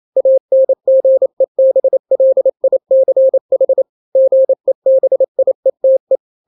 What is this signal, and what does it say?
Morse code, heard as a keyed tone, about 28 wpm, 530 Hz. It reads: ANGEBLICH GEBIETE
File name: daily / 27wpm